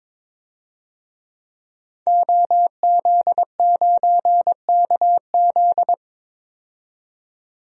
Morse code, heard as a keyed tone, about 22 wpm, 700 Hz. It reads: OZ9KZ
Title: DXCC-100 Call Signs - 5 Characters - 22wpm